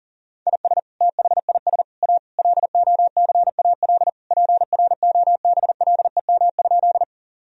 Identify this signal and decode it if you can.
Morse code, heard as a keyed tone, about 40 wpm, 700 Hz. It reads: IS THIS A LOCAL PROBLEM?
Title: Sentences from Top 1000 Words - Review Part 3 - 40wpm